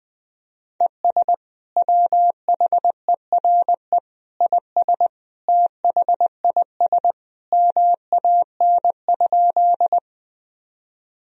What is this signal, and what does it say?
Morse code, heard as a keyed tone, about 21 wpm, 700 Hz. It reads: ES WHERE IS THIS MAN?
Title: Sentences from Top 200 Words and Abbreviations 20wpm